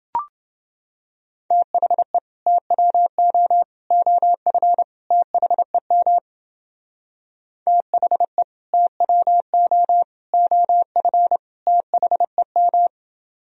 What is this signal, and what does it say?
Morse code, heard as a keyed tone, about 29 wpm, 700 Hz. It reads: THE TWO OF THEM THE TWO OF THEM